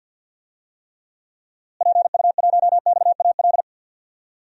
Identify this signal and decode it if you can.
Morse code, heard as a keyed tone, about 47 wpm, 700 Hz. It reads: PU1XAL
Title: DXCC-100 Call Signs - 6 Characters - Repeated 3x - 50wpm